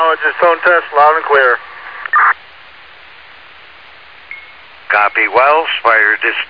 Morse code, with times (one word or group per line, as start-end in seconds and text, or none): none